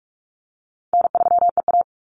Morse code, heard as a keyed tone, about 44 wpm, 700 Hz. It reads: N3EA
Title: Easy Call Signs Repeated 2x - 45wpm